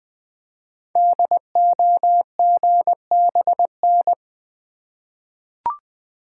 Morse code, heard as a keyed tone, about 20 wpm, 700 Hz. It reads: DOGBN
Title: Code Groups - 5 Letters - 20wpm